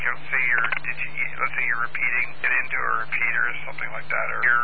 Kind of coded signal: Emergency Alert System (SAME) tones